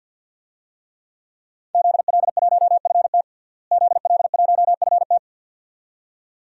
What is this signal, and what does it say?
Morse code, heard as a keyed tone, about 52 wpm, 700 Hz. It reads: ZL1FT ZL1FT
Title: DXCC-100 Call Signs - 5 Characters - Repeated 2x - 50wpm